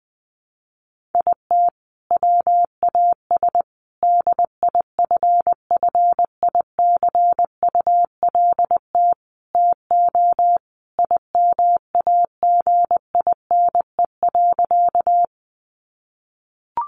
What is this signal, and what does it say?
Morse code, heard as a keyed tone, about 20 wpm, 700 Hz. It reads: IT WAS DIFFICULT TO IMAGINE.
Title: Sentences from Top 1400 Words 20wpm